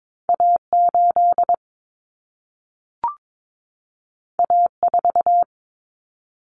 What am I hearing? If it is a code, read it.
Morse code, heard as a keyed tone, about 22 wpm, 700 Hz. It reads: A8 A4
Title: Common Cut Numbers - 2-Digit 22wpm